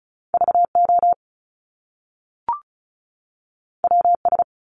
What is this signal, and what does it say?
Morse code, heard as a keyed tone, about 36 wpm, 700 Hz. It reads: VO WS
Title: Top Two Letter Combinations in Words 35wpm